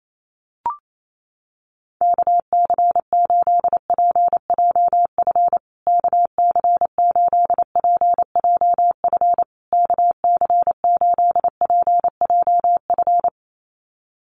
Morse code, heard as a keyed tone, about 28 wpm, 700 Hz. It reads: KC8PJF KC8PJF KC8PJF
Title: USA Call Signs Repeated 3x 28wpm